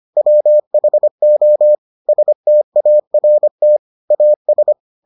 Morse code, heard as a keyed tone, about 25 wpm, 600 Hz. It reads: WHO START AS